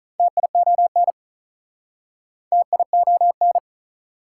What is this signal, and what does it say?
Morse code, heard as a keyed tone, about 47 wpm, 700 Hz. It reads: TION TION